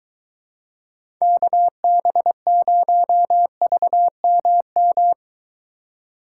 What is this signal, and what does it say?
Morse code, heard as a keyed tone, about 22 wpm, 700 Hz. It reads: KB0VMM